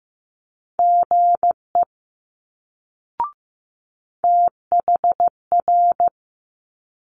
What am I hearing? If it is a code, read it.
Morse code, heard as a keyed tone, about 15 wpm, 700 Hz. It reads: GE THR
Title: Multi-letter Phonemes Quick Fire 15wpm